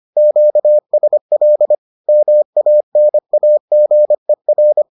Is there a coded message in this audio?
Yes: Morse code from 0.2 to 4.8 s, heard as a keyed tone, about 25 wpm, 600 Hz. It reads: QSL MANAGER